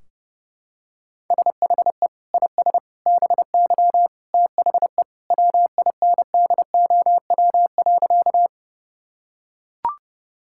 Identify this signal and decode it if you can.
Morse code, heard as a keyed tone, about 30 wpm, 700 Hz. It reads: SHE IS BY THE WINDOW.
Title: Sentences from Top 1600 Words 30wpm